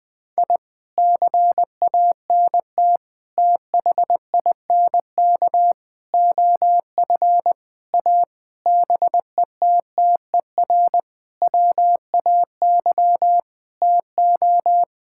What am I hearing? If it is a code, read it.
Morse code, heard as a keyed tone, about 21 wpm, 700 Hz. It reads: I CANT THINK OF A BETTER WAY TO
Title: Sentences with 8-Word Phrases - Spoken Beginning - 20wpm